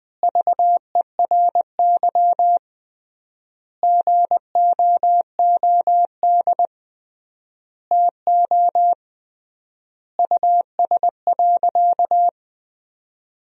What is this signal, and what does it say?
Morse code, heard as a keyed tone, about 20 wpm, 700 Hz. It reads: VERY GOOD TO US.
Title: 3x Word-Spacing - Sentences from Top 200 Words - 20wpm